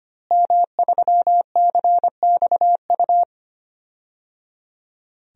Morse code, heard as a keyed tone, about 25 wpm, 700 Hz. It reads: M3CXU
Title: DXCC-100 Call Signs - 5 Characters - Repeated 3x - 25wpm